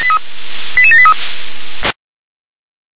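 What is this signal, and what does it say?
Five-tone selective call (ZVEI), heard as quick runs of five tones, tones of about 70 ms. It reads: 88783
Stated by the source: ZVEI five-tone selective call received on an SDR.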